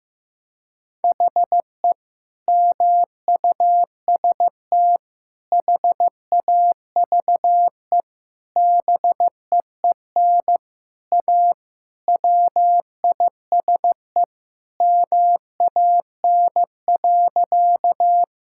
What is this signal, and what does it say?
Morse code, heard as a keyed tone, about 15 wpm, 700 Hz. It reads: HE MUST HAVE BEEN A WISE MAN.
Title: Sentences from Top 1600 Words 15wpm